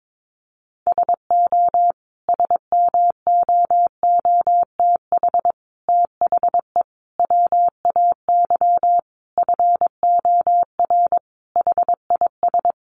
Morse code, heard as a keyed tone, about 22 wpm, 700 Hz. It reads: SO SMOOTH THE WAY FOR HIS
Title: Sentences from Top 1400 Words 22wpm